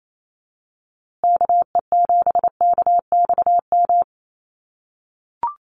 Morse code, heard as a keyed tone, about 28 wpm, 700 Hz. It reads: KE7KXM
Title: USA Call Signs Repeated 2x 28wpm